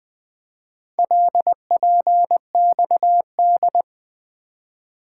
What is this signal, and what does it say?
Morse code, heard as a keyed tone, about 20 wpm, 700 Hz. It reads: LPXD
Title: Four Letter-Number Combinations - 20wpm